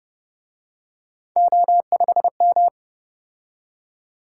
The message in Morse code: O5M

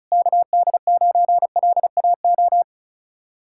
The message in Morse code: KD9LAO

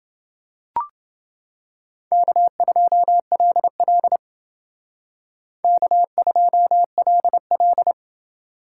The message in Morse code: K2LL K2LL

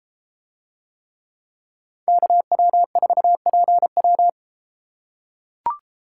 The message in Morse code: KW4PW

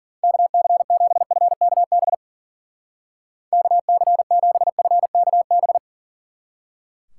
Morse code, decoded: KC7FKB KC7FKB

